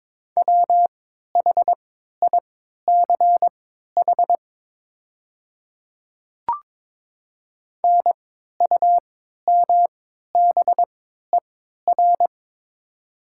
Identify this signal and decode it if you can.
Morse code, heard as a keyed tone, about 22 wpm, spaced as for 12 wpm, 700 Hz. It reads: WHICH NUMBER
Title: 3x Character Spacing - Top 100 Words - 22wpm